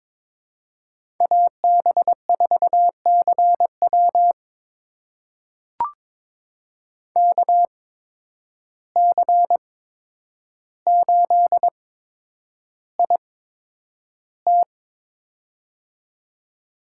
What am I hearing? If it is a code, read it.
Morse code, heard as a keyed tone, about 22 wpm, spaced as for 9 wpm, 700 Hz. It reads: AB4CW KC8IT
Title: USA Call Signs - 5 Characters - 8x Character Spacing - 22wpm